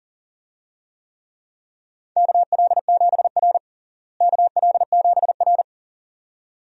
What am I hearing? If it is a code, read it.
Morse code, heard as a keyed tone, about 40 wpm, 700 Hz. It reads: KL7R KL7R
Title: DXCC-100 Call Signs - 4 Characters - Repeated 2x - 40wpm